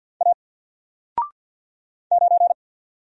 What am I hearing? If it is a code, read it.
Morse code, heard as a keyed tone, about 48 wpm, 700 Hz. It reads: A 9
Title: Single Letter-Number Alt 50wpm